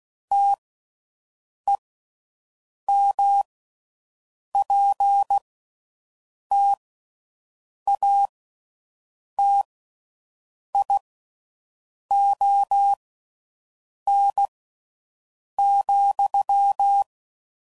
Morse code, heard as a keyed tone, about 16 wpm, spaced as for 6 wpm, 800 Hz. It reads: TEMPTATION,